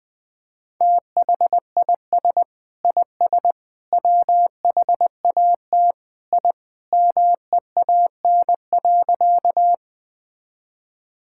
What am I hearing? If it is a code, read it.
Morse code, heard as a keyed tone, about 20 wpm, 700 Hz. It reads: THIS IS WHAT I MEAN.
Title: Sentences with 3-Word Phrases - 20wpm